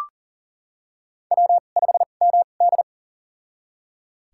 Morse code, heard as a keyed tone, about 38 wpm, spaced as for 28 wpm, 690 Hz. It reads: W5MD
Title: USA Call Signs - 4 Characters - 2x Character Spacing - 40wpm